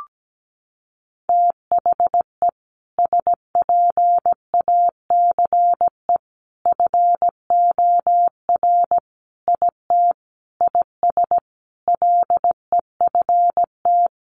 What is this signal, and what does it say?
Morse code, heard as a keyed tone, about 17 wpm, 700 Hz. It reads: THE SPACE FOR IT IS LEFT
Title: Sentences from Top 1500 Words 17wpm